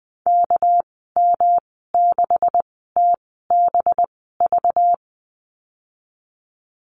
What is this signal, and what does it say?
Morse code, heard as a keyed tone, about 20 wpm, spaced as for 15 wpm, 700 Hz. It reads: KM6TBV
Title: 200 Lesson 32g - Letter k - Call Signs 20wpm